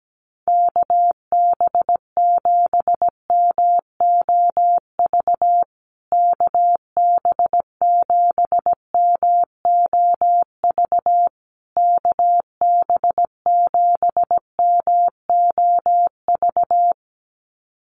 Morse code, heard as a keyed tone, about 17 wpm, 700 Hz. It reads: KB7MOV KB7MOV KB7MOV
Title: USA Call Signs Repeated 3x 17wpm